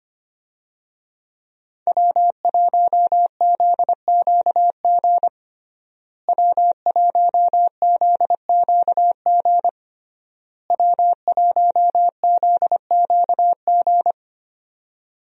Morse code, heard as a keyed tone, about 25 wpm, 700 Hz. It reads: W1ZQG W1ZQG W1ZQG
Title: USA Call Signs - 5 Characters - Repeated 3x - 25wpm